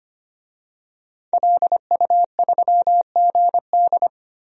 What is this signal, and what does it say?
Morse code, heard as a keyed tone, about 25 wpm, 700 Hz. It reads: LU3GD
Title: DXCC-100 Call Signs - 5 Characters - Repeated 2x - 25wpm